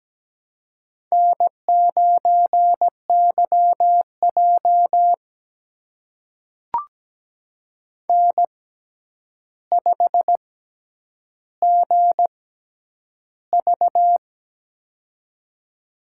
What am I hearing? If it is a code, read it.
Morse code, heard as a keyed tone, about 17 wpm, spaced as for 9 wpm, 700 Hz. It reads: N9YJ N5GV
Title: USA Call Signs - 4 Characters - 6x Character Spacing - 17wpm